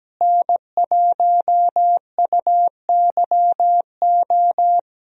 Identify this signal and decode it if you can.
Morse code, heard as a keyed tone, about 17 wpm, 700 Hz. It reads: N1UYO
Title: USA Call Signs - 5 Characters - Repeated 3x - 17wpm